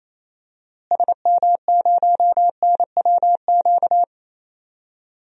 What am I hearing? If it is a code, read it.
Morse code, heard as a keyed tone, about 28 wpm, 700 Hz. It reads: SM0NWQ
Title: DXCC-100 Call Signs - 6 Characters - Repeated 3x - 28wpm